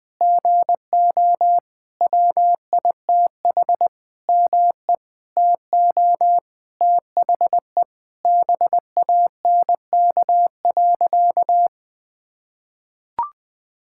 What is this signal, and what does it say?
Morse code, heard as a keyed tone, about 20 wpm, 700 Hz. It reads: GO WITH ME TO THE BANK.
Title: Sentences from Top 1000 Words - Review Part 6 - 20wpm